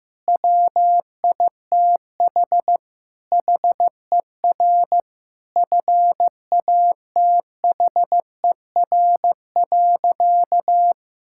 Morse code, heard as a keyed tone, about 15 wpm, 700 Hz. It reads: WITH HER FATHER.
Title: Sentences from Top 1000 Words - Review Part 5 - 15wpm